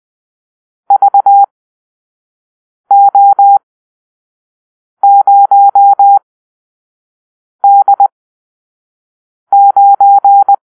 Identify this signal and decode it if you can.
Morse code, heard as a keyed tone, about 20 wpm, spaced as for 5 wpm, 800 Hz. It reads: VO0D9
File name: arrl / Grp-7 Drill-3